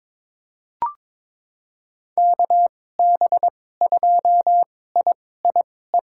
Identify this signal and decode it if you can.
Morse code, heard as a keyed tone, about 22 wpm, spaced as for 16 wpm, 700 Hz. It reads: KB2IIE
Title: USA Call Signs - 6 Characters - 2x Character Spacing - 22wpm